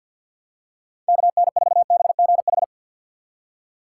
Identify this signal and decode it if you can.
Morse code, heard as a keyed tone, about 52 wpm, 700 Hz. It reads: KN4BGH
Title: USA Call Signs - 6 Characters - 4x Character Spacing - 50wpm